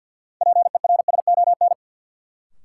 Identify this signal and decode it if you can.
Morse code, heard as a keyed tone, about 52 wpm, 700 Hz. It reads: PERSON
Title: Sentences from Top 400 Words - New Word Drill - 50wpm